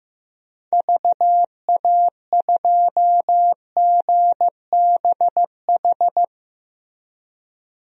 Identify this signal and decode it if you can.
Morse code, heard as a keyed tone, about 16 wpm, 700 Hz. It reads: VA2GBH
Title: DXCC-100 Call Signs - 6 Characters - Repeated 3x - 15wpm